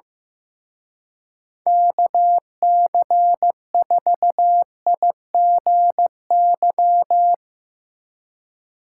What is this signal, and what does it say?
Morse code, heard as a keyed tone, about 16 wpm, 700 Hz. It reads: KC4IGY
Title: Speed Racing - USA Call Signs 15wpm